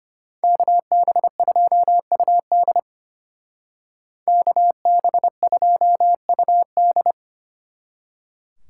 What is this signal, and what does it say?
Morse code, heard as a keyed tone, about 29 wpm, 700 Hz. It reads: KB2UD KB2UD
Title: Speed Racing - USA Call Signs 25wpm